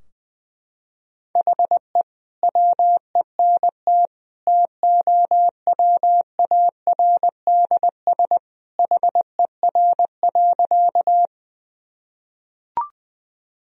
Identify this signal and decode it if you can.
Morse code, heard as a keyed tone, about 20 wpm, 700 Hz. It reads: HE WENT TOWARDS HER.
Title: Sentences from Top 1000 Words - Review Part 2 - 20wpm